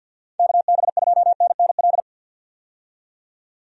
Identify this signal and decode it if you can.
Morse code, heard as a keyed tone, about 52 wpm, 700 Hz. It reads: KB2NNL
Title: USA Call Signs - 6 Characters - 50wpm